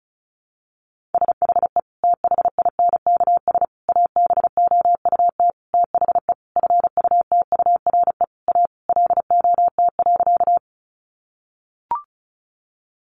Morse code, heard as a keyed tone, about 34 wpm, 700 Hz. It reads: SHE THINKS ABOUT THE FUTURE A LOT.
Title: Easy Phrases 35wpm